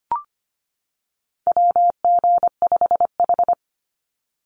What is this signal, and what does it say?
Morse code, heard as a keyed tone, about 25 wpm, 700 Hz. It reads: WG5H
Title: Call Signs - 25wpm